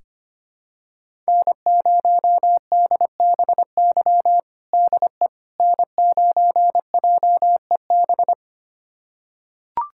Morse code, heard as a keyed tone, about 25 wpm, 700 Hz. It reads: N0DBY DE N9JEB
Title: USA Callsign DE Callsign 25wpm